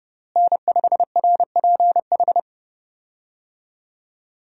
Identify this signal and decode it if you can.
Morse code, heard as a keyed tone, about 32 wpm, 700 Hz. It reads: N5RPH
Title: USA Call Signs - 5 Characters - Repeated 2x - 30wpm